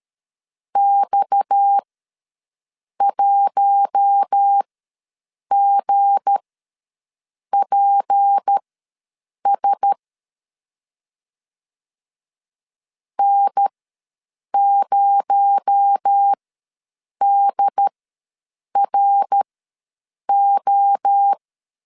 Morse code, heard as a keyed tone, about 13 wpm, spaced as for 7 wpm, 800 Hz. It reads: X1GPS N0DRO
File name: LEZIONI / 7 GF9Z / 7 G ws 12wpm 3140